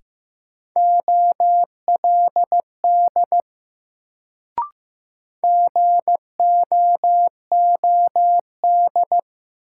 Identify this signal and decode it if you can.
Morse code, heard as a keyed tone, about 15 wpm, 700 Hz. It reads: OLD GOOD
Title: Sentences from Top 200 Words - New Word Drill - 15wpm